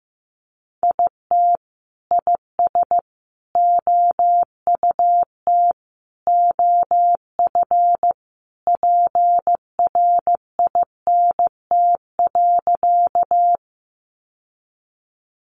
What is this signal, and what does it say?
Morse code, heard as a keyed tone, about 15 wpm, 700 Hz. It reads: IT IS OUT OF PRINT.
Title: Sentences from Top 1400 Words 15wpm